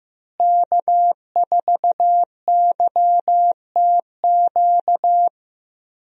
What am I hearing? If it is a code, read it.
Morse code, heard as a keyed tone, about 15 wpm, 700 Hz. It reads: K4YTQ
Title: USA Call Signs - 5 Characters - Repeated 3x - 15wpm